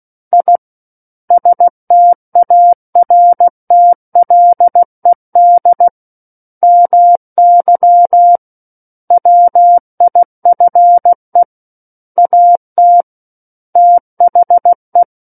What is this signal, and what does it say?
Morse code, heard as a keyed tone, about 16 wpm, 700 Hz. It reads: I STARTLED MY WIFE AT THE
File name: Book One / 07-Chapter 7 - 16 WPM